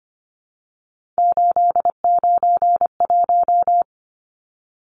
Morse code, heard as a keyed tone, about 25 wpm, 690 Hz. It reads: 891
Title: Numbers - 3-Digit 25wpm